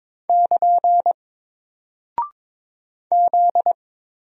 Morse code, heard as a keyed tone, about 22 wpm, 700 Hz. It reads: <KN> Z
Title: Single Character (Letter:Number:Punctuation:Prosign) Alt 22wpm